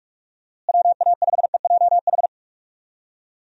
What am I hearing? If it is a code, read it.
Morse code, heard as a keyed tone, about 46 wpm, 700 Hz. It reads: WA5EJH
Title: USA Call Signs - 6 Characters - Repeated 2x - 45wpm